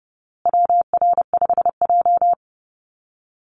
Morse code, heard as a keyed tone, about 30 wpm, 700 Hz. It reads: WR5J
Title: Call Signs Repeated 2x - 30wpm